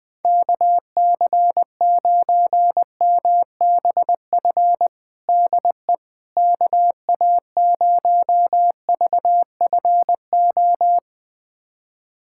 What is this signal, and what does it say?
Morse code, heard as a keyed tone, about 21 wpm, 700 Hz. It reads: KC9MBF DE KA0VFO
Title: USA Callsign DE Callsign - 6 Characters - 20wpm